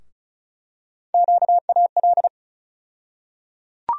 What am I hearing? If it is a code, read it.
Morse code, heard as a keyed tone, about 34 wpm, 700 Hz. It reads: QAL
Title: USA Call Sign Postfixes - 3 Letters 35wpm